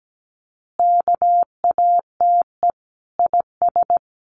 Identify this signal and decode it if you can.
Morse code, heard as a keyed tone, about 17 wpm, 700 Hz. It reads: KATE IS
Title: Easy Phrases 17wpm